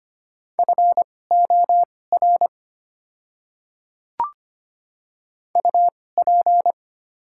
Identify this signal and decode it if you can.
Morse code, heard as a keyed tone, about 25 wpm, spaced as for 18 wpm, 690 Hz. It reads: FOR UP
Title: 2x Character Spacing - Top 200 Words - 25wpm